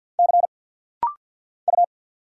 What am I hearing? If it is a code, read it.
Morse code, heard as a keyed tone, about 49 wpm, 700 Hz. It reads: C U
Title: Single Letter-Number Alt - Rapid-Fire 50wpm